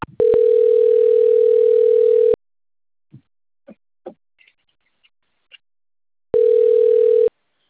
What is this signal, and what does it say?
Ringback tone (North America), 440 + 480 Hz.